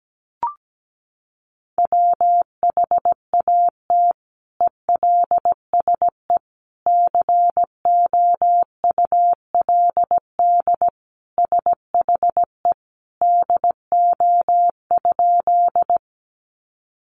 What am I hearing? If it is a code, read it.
Morse code, heard as a keyed tone, about 17 wpm, 700 Hz. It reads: WHAT ELSE COULD SHE DO?
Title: Sentences from Top 800 Words 17wpm